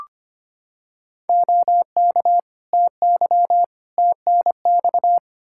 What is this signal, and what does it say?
Morse code, heard as a keyed tone, about 25 wpm, 700 Hz. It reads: OK TY TNX
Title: K1USN SST QSO Practice 25wpm